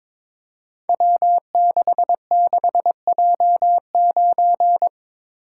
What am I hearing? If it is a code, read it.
Morse code, heard as a keyed tone, about 22 wpm, 700 Hz. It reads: W66J9
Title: Code Groups - 5 Letter:Numbers - 22wpm